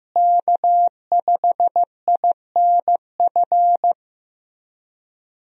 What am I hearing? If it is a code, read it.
Morse code, heard as a keyed tone, about 15 wpm, 700 Hz. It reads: K5INF
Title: USA Call Signs - 5 Characters - Repeated 3x - 15wpm